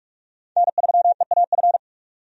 Morse code, heard as a keyed tone, about 42 wpm, 700 Hz. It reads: N3EAF
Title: USA Call Signs - 5 Characters - 45wpm